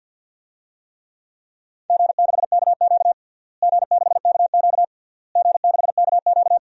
Morse code, heard as a keyed tone, about 52 wpm, 700 Hz. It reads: G6KQ G6KQ G6KQ
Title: DXCC-100 Call Signs - 4 Characters - Repeated 3x - 50wpm